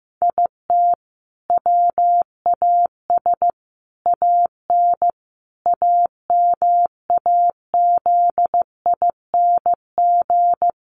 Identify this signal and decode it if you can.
Morse code, heard as a keyed tone, about 15 wpm, 700 Hz. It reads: IT WAS AN AMAZING
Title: Sentences from Top 1500 Words 15wpm